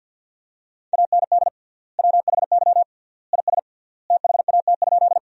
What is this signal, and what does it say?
Morse code, heard as a keyed tone, about 53 wpm, 700 Hz. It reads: AND WHY IS THAT?
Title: 3x Word-Spacing - Sentences from Top 200 Words - 50wpm